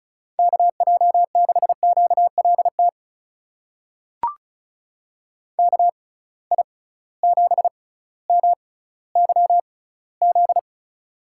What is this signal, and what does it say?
Morse code, heard as a keyed tone, about 34 wpm, spaced as for 18 wpm, 700 Hz. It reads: KJ6QLT KI7MYZ
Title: USA Call Signs - 6 Characters - 6x Character Spacing - 35wpm